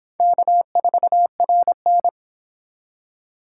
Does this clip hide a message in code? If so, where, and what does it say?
Yes: Morse code from 0.2 to 2.1 s, heard as a keyed tone, about 28 wpm, 700 Hz. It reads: K4RN